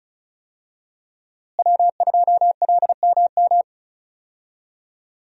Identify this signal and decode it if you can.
Morse code, heard as a keyed tone, about 34 wpm, 700 Hz. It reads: W2LMM
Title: USA Call Signs - 5 Characters - Repeated 2x - 35wpm